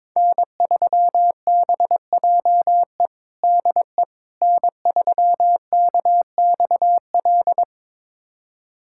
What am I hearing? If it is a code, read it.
Morse code, heard as a keyed tone, about 21 wpm, 700 Hz. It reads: N3BJE DE N3KXL
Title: USA Callsign DE Callsign - 5 Characters - 22wpm